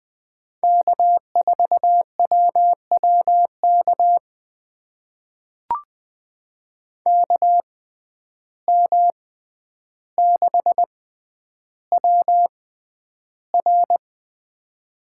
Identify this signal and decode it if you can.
Morse code, heard as a keyed tone, about 20 wpm, spaced as for 10 wpm, 700 Hz. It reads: K4WWK KM6WR
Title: USA Call Signs - 5 Characters - 6x Character Spacing - 20wpm